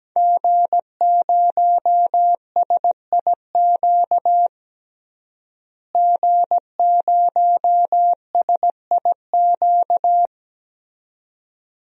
Morse code, heard as a keyed tone, about 17 wpm, 700 Hz. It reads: G0SIQ G0SIQ
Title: DXCC-100 Call Signs - 5 Characters - Repeated 2x - 17wpm